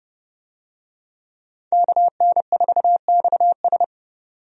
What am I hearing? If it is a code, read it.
Morse code, heard as a keyed tone, about 30 wpm, 700 Hz. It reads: KN4XS
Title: USA Call Signs - 5 Characters - 30wpm